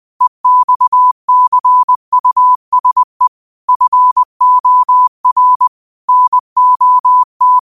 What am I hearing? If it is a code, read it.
Morse code, heard as a keyed tone, about 20 wpm, 1000 Hz. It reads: EXCUSE FOR NOT